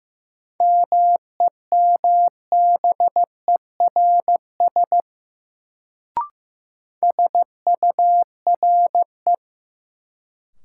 Morse code, heard as a keyed tone, about 15 wpm, 700 Hz. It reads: MEMBERS SURE
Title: Sets of 3 Words - New Word Drill - 15wpm